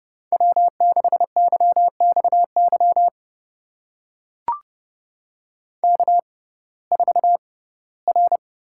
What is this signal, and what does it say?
Morse code, heard as a keyed tone, about 29 wpm, spaced as for 18 wpm, 700 Hz. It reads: W6YXY K4R
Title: USA Call Signs - 5 Characters - 6x Character Spacing - 30wpm